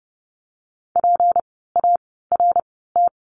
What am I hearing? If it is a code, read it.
Morse code, heard as a keyed tone, about 30 wpm, spaced as for 17 wpm, 700 Hz. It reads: PART
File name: course / 30:10wpm / 237 Lesson 38f - Letter q - Words 30wpm